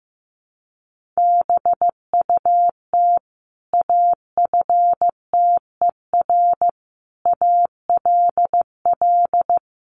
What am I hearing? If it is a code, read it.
Morse code, heard as a keyed tone, about 15 wpm, 700 Hz. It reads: BUT AFTER ALL
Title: Sentences from Top 1500 Words 15wpm